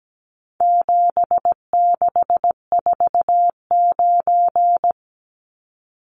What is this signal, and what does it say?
Morse code, heard as a keyed tone, about 17 wpm, 700 Hz. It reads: 7649
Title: Numbers - 4-Digit 17wpm